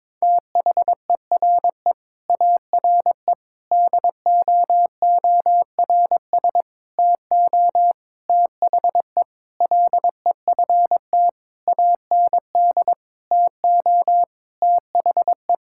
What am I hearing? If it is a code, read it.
Morse code, heard as a keyed tone, about 22 wpm, 700 Hz. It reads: THERE ARE DOORS TO THE LEFT AND TO THE
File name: practice / 22wpm / Sentences with 8-Word Phrases - 22wpm